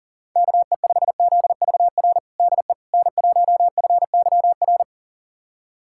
Morse code, heard as a keyed tone, about 40 wpm, 700 Hz. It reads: KE5ZVR DE N1FYR
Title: USA Callsign DE Callsign 40wpm